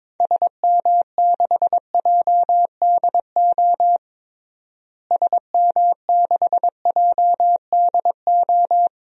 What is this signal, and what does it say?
Morse code, heard as a keyed tone, about 22 wpm, 700 Hz. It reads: SM6JDO SM6JDO
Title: DXCC-100 Call Signs - 6 Characters - Repeated 3x - 22wpm